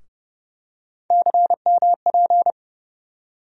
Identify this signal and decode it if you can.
Morse code, heard as a keyed tone, about 31 wpm, 700 Hz. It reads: CMP
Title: USA Call Sign Postfixes - 3 Letters 30wpm